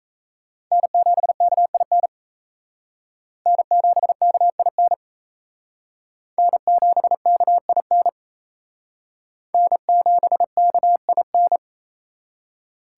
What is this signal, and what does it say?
Morse code, heard as a keyed tone, about 36 wpm, 700 Hz. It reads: N7KIN N7KIN N7KIN N7KIN